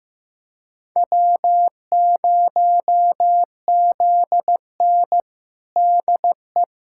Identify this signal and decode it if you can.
Morse code, heard as a keyed tone, about 15 wpm, 700 Hz. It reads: W0ZN DE
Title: USA Callsign DE Callsign - 4 Characters - 15wpm